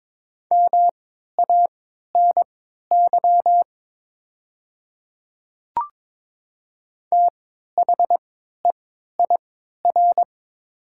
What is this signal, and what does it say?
Morse code, heard as a keyed tone, about 22 wpm, spaced as for 12 wpm, 700 Hz. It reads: MANY THEIR
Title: 3x Character Spacing - Top 100 Words - 22wpm